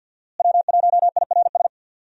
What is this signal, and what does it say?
Morse code, heard as a keyed tone, about 52 wpm, 700 Hz. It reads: W1IRS
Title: USA Call Signs - 5 Characters - Repeated 2x - 50wpm